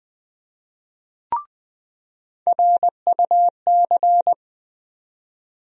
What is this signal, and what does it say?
Morse code, heard as a keyed tone, about 20 wpm, 700 Hz. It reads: RUC
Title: USA Call Sign Postfixes - 3 Letters 20wpm